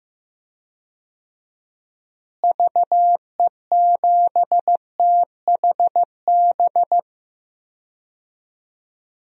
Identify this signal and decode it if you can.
Morse code, heard as a keyed tone, about 15 wpm, 700 Hz. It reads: VE7THB
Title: DXCC-100 Call Signs - 6 Characters - Repeated 2x - 15wpm